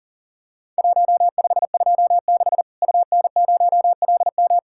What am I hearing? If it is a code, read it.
Morse code, heard as a keyed tone, about 37 wpm, 700 Hz. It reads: 1526 UN0LM
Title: Worked All Europe DX Contest - QTC Exchange - 40wpm